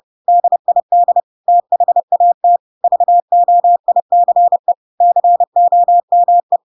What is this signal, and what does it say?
Morse code, heard as a keyed tone, about 30 wpm, 700 Hz. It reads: DID THAT VOICE COME
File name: Pinocchio / Pinocchio ch1 30 30 ik1hge